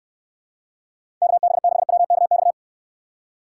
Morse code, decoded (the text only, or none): KB6WKX